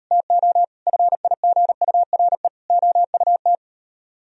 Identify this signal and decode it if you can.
Morse code, heard as a keyed tone, about 38 wpm, 700 Hz. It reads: TO FIGURE OUT